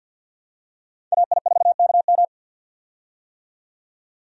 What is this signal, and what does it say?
Morse code, heard as a keyed tone, about 51 wpm, 700 Hz. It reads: AI4KM
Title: USA Call Signs - 5 Characters - 50wpm